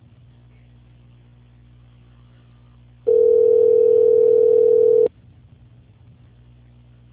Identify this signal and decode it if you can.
Ringback tone (North America), 440 + 480 Hz.
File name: out- -2037- - - .24756